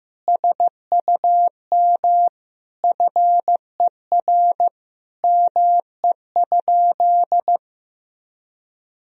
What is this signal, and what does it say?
Morse code, heard as a keyed tone, about 15 wpm, 700 Hz. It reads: SUM FER ME?
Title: Sentences from Top 200 Words and Abbreviations 15wpm